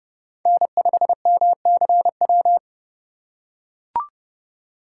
Morse code, heard as a keyed tone, about 29 wpm, 700 Hz. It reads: N5MCW